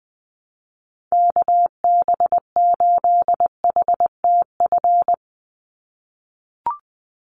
Morse code, heard as a keyed tone, about 20 wpm, 700 Hz. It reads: KB8HTF
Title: USA Call Signs 20wpm